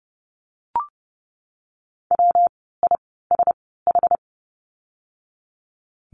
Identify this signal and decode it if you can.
Morse code, heard as a keyed tone, about 30 wpm, spaced as for 17 wpm, 700 Hz. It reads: WISH
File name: course / 30:10wpm / 115 Lesson 19f - Letter w - Words 30wpm